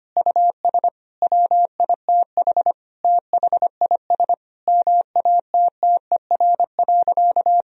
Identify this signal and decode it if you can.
Morse code, heard as a keyed tone, about 25 wpm, 700 Hz. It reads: US WITH THIS MATTER.
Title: Sentences from Top 1000 Words - Review Part 2 - 25wpm